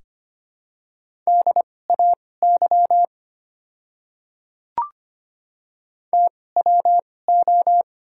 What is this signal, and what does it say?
Morse code, heard as a keyed tone, about 25 wpm, spaced as for 18 wpm, 700 Hz. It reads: DAY TWO
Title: 2x Character Spacing - Top 100 Words - 25wpm